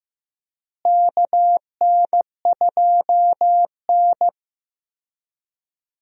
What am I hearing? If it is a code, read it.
Morse code, heard as a keyed tone, about 15 wpm, 700 Hz. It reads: KN2N
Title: USA Call Signs - 4 Characters - Repeated 3x - 15wpm